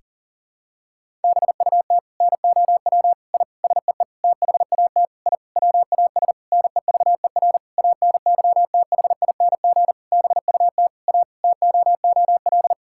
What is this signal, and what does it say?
Morse code, heard as a keyed tone, about 40 wpm, 700 Hz. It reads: BUT NOW I SEE THAT I WAS NEVER ANYTHING BUT A TOOL
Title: Sentences from Top 1000 Words - Review Part 6 - 40wpm